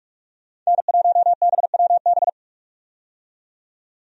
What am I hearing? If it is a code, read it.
Morse code, heard as a keyed tone, about 43 wpm, 700 Hz. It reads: N1BWB
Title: USA Call Signs - 5 Characters - 45wpm